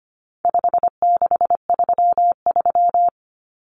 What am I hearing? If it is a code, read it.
Morse code, heard as a keyed tone, about 25 wpm, 700 Hz. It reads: 5633
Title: Numbers - 4-Digit 25wpm